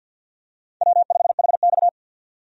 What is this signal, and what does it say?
Morse code, heard as a keyed tone, about 52 wpm, 700 Hz. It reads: W5HX
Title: USA Call Signs - 4 Characters - 50wpm